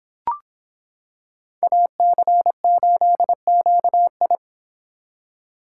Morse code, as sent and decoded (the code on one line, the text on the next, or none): .- -.-. ---.. --.- ..
AC8QI